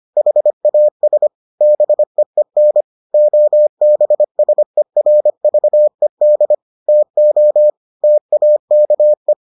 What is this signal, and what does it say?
Morse code, heard as a keyed tone, about 25 wpm, 600 Hz. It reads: HAS BEEN OBSERVED TO TAKE